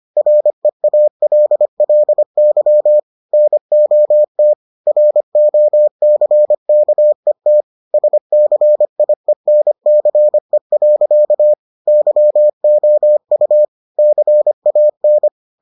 Morse code, heard as a keyed tone, about 25 wpm, 600 Hz. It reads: REALLY NOT ROCKET SCIENCE. YOU CAN